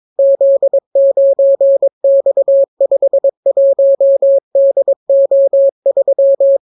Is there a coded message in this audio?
Yes: Morse code from 0.2 to 6.6 s, heard as a keyed tone, about 22 wpm, 550 Hz. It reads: Z9X51DO3